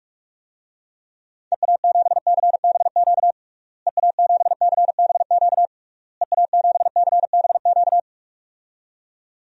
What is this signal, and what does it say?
Morse code, heard as a keyed tone, about 46 wpm, 700 Hz. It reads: EA7CBQ EA7CBQ EA7CBQ
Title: DXCC-100 Call Signs - 6 Characters - Repeated 3x - 45wpm